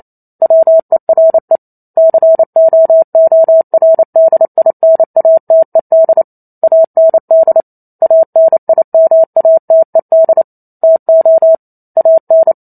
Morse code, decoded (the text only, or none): WERE COORDINATED AND ANIMATED TO AN